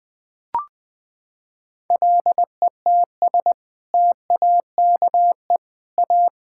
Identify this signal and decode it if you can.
Morse code, heard as a keyed tone, about 20 wpm, 700 Hz. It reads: LETS TAKE A
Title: Sets of 6 Words - 20wpm